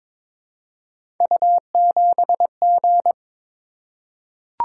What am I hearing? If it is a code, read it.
Morse code, heard as a keyed tone, about 22 wpm, 700 Hz. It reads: U7G
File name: practice / 22wpm / Three Letter-Number Combinations - 22wpm